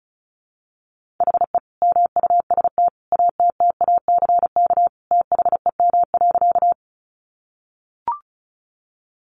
Morse code, heard as a keyed tone, about 34 wpm, 700 Hz. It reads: HE MUST ATTACK THEM.
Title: Sentences from Top 1500 Words 35wpm